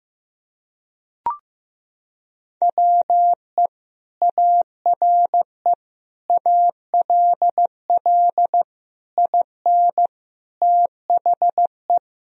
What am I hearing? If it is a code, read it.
Morse code, heard as a keyed tone, about 15 wpm, 700 Hz. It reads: WE ARE ALL IN THE
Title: Sentences from Top 1000 Words - Review Part 3 - 15wpm